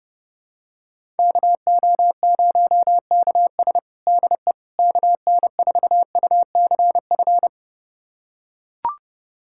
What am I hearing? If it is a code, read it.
Morse code, heard as a keyed tone, about 32 wpm, 700 Hz. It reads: KO0KS DE KN4UCF
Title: USA Callsign DE Callsign 30wpm